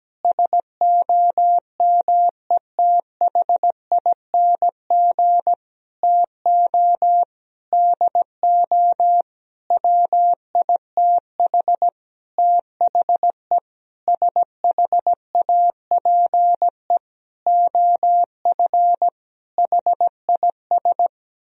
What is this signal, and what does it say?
Morse code, heard as a keyed tone, about 17 wpm, 700 Hz. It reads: SOMETHING TO DO WITH THE SHAPE OF HIS
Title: Sentences from Top 1000 Words - Review Part 3 - 17wpm